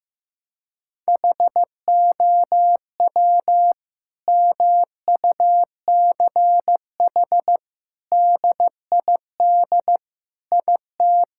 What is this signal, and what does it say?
Morse code, heard as a keyed tone, about 15 wpm, 700 Hz. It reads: HOW MUCH DID IT
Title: Sentences from Top 1000 Words - Review Part 5 - 15wpm